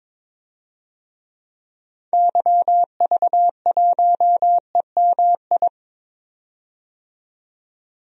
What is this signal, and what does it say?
Morse code, heard as a keyed tone, about 22 wpm, 700 Hz. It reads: YV1EMI
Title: DXCC-100 Call Signs - 6 Characters - 22wpm